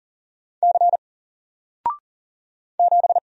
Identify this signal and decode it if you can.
Morse code, heard as a keyed tone, about 40 wpm, 700 Hz. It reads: C 7
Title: Single Letter-Number 40wpm